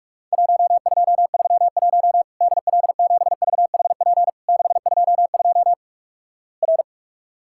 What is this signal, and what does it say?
Morse code, heard as a keyed tone, about 44 wpm, 700 Hz. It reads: 1231 DL7VHP 622 R
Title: Worked All Europe DX Contest - QTC Exchange - 45wpm